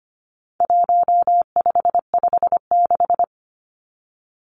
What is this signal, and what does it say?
Morse code, heard as a keyed tone, about 25 wpm, 700 Hz. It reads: 1556
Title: Numbers - 4-Digit 25wpm